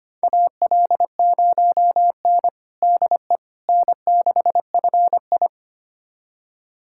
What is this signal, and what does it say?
Morse code, heard as a keyed tone, about 25 wpm, 700 Hz. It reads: AL0N DE N6FI
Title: USA Callsign DE Callsign - 4 Characters - 25wpm